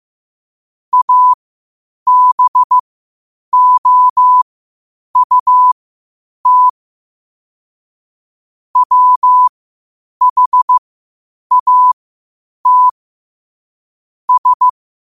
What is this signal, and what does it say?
Morse code, heard as a keyed tone, about 15 wpm, spaced as for 8 wpm, 1000 Hz. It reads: ABOUT WHAT S
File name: qotdfiles / 8wpm